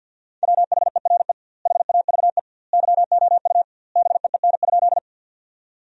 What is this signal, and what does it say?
Morse code, heard as a keyed tone, about 52 wpm, 700 Hz. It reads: WHERE HAVE YOU BEEN?
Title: 2x Word-Spacing - Sentences from Top 200 Words - 50wpm